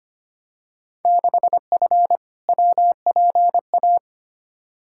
Morse code, heard as a keyed tone, about 25 wpm, 700 Hz. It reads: 6F WPA
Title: Field Day Exchanges Encore - 25wpm